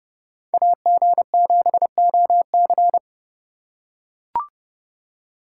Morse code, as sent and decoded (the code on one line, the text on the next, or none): .- --. --... --- -.-.
AG7OC